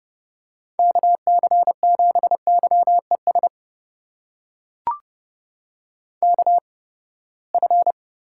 Morse code, heard as a keyed tone, about 30 wpm, spaced as for 21 wpm, 700 Hz. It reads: KC7YES KF
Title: USA Call Signs - 6 Characters - 8x Character Spacing - 30wpm